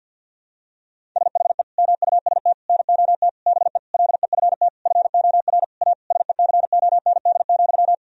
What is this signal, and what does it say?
Morse code, heard as a keyed tone, about 51 wpm, 700 Hz. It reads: SHE MUST NOT BE LEFT FOR A SECOND,
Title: Sentences from Top 1000 Words - Review Part 1 - 50wpm